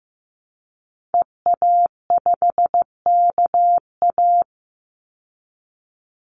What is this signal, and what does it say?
Morse code, heard as a keyed tone, about 15 wpm, 700 Hz. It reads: EA5KA
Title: Call Signs Repeated 2x - 15wpm